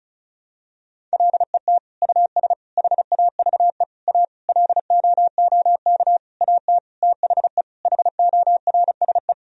Morse code, heard as a keyed tone, about 36 wpm, 700 Hz. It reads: LET US HAVE A LOOK AT THE HORSE
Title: Sentences from Top 1000 Words - Review Part 1 - 35wpm